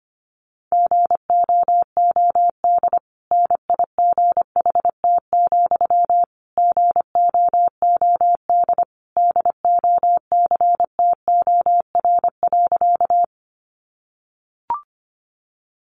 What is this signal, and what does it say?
Morse code, heard as a keyed tone, about 25 wpm, 700 Hz. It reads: GOOD NIGHT, GOOD DOCTOR.
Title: Sentences from Top 1500 Words 25wpm